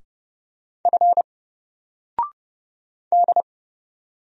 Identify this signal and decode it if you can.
Morse code, heard as a keyed tone, about 30 wpm, 700 Hz. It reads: F D
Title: Single Letter-Number 30wpm